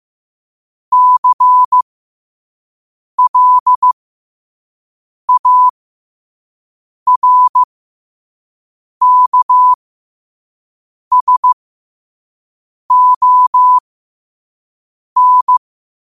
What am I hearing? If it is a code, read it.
Morse code, heard as a keyed tone, about 15 wpm, spaced as for 5 wpm, 1000 Hz. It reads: CLARKSON
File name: qotdfiles / 5wpm